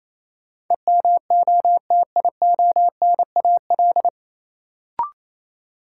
Morse code, heard as a keyed tone, about 28 wpm, 700 Hz. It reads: EMOTIONAL
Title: Sentences from Top 1000 Words - Review Part 5 - New Word Drill - 28wpm